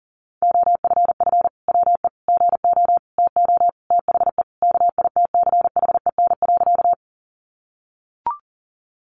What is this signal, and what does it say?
Morse code, heard as a keyed tone, about 40 wpm, 700 Hz. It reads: OFF WE GO TO THE KITCHEN.
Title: Sentences from Top 1400 Words 40wpm